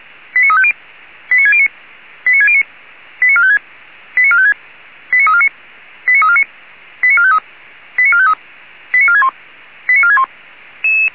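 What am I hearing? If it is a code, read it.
Five-tone selective call (ZVEI), heard as quick runs of five tones, tones of about 70 ms. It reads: 89350 89609 89609 89456 89456 89349 89349 89453 89453 89461 89461
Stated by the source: ZVEI five-tone selective call received by SDR, server of Feuerwehr Orlamünde fire brigade.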